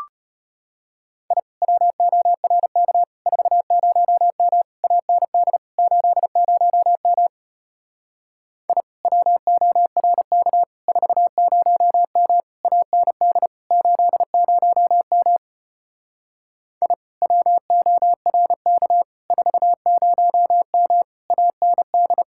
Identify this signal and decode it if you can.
Morse code, heard as a keyed tone, about 32 wpm, 700 Hz. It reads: I WORK 40M AND 80M I WORK 40M AND 80M I WORK 40M AND